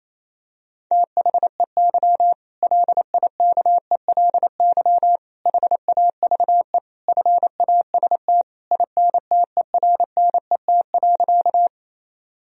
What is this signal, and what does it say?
Morse code, heard as a keyed tone, about 28 wpm, 700 Hz. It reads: THEY LIKELY HAVE FAST INTERNET.
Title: Sentences from Top 1000 Words - Review Part 4 - 28wpm